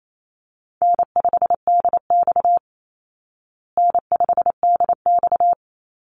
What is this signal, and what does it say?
Morse code, heard as a keyed tone, about 28 wpm, 700 Hz. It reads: N5DX N5DX
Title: Call Signs Repeated 3x - 28wpm